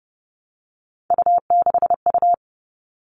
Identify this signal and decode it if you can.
Morse code, heard as a keyed tone, about 30 wpm, 700 Hz. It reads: U6U
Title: All Cut Numbers - 3-Digit 30wpm